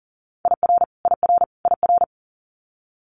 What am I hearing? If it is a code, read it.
Morse code, heard as a keyed tone, about 42 wpm, 700 Hz. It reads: IR IR IR
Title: Multi-letter Phonemes Repeated 3x 40wpm